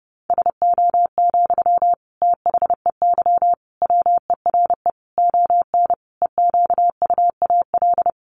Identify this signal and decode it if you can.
Morse code, heard as a keyed tone, about 30 wpm, 700 Hz. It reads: SO, THEY WERE ON EQUAL
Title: Sentences from Top 1400 Words 30wpm